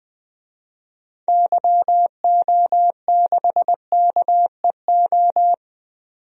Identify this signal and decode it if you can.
Morse code, heard as a keyed tone, about 20 wpm, 700 Hz. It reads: YO6KEO
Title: DXCC-100 Call Signs - 6 Characters - 20wpm